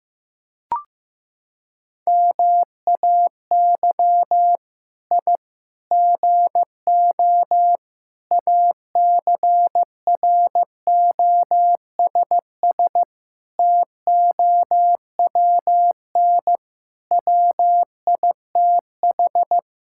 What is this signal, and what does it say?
Morse code, heard as a keyed tone, about 15 wpm, 700 Hz. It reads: MAY I GO ACROSS TOWN WITH
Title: Sentences from Top 1000 Words - Review Part 5 - 15wpm